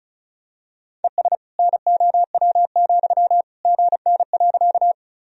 Morse code, heard as a keyed tone, about 36 wpm, 700 Hz. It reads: ES NOW, GN.
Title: Sentences from Top 300 Words and Abbreviations 35wpm